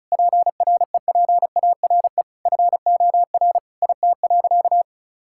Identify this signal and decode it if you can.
Morse code, heard as a keyed tone, about 36 wpm, 700 Hz. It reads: PREPARE FOR IT.
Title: Sentences from Top 1000 Words - Review Part 6 - 35wpm